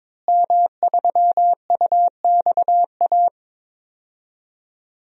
Morse code, heard as a keyed tone, about 22 wpm, 700 Hz. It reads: M3UXA
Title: DXCC-100 Call Signs - 5 Characters - Repeated 2x - 22wpm